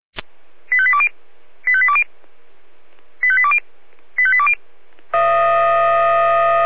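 Five-tone selective call (ZVEI), heard as quick runs of five tones, tones of about 70 ms. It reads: 85820 85820 85820 85820
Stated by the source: Received on an SDR.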